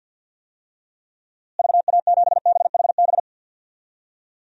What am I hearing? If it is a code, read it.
Morse code, heard as a keyed tone, about 48 wpm, 700 Hz. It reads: VA7BHB
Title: DXCC-100 Call Signs - 6 Characters - Repeated 3x - 50wpm